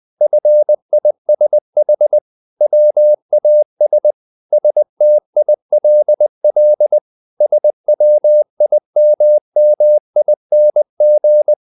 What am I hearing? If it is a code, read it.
Morse code, heard as a keyed tone, about 20 wpm, 600 Hz. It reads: FISH WAS STILL SWIMMING
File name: the old man and the sea / the old man and the sea part 1 20wpm